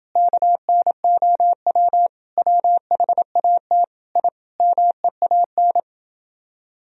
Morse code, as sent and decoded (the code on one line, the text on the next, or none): -.- -. --- .-- / .-- .... .- - / .. / -- . .- -.
KNOW WHAT I MEAN